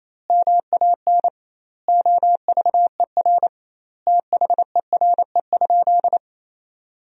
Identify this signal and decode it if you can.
Morse code, heard as a keyed tone, about 27 wpm, 700 Hz. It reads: MAN OVER THERE?
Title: 2x Word-Spacing - Sentences from Top 200 Words - 28wpm